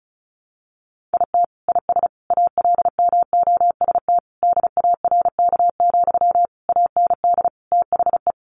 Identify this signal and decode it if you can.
Morse code, heard as a keyed tone, about 34 wpm, 700 Hz. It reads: IT IS ALMOST DARK, AND THE
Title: Sentences from Top 1200 Words 35wpm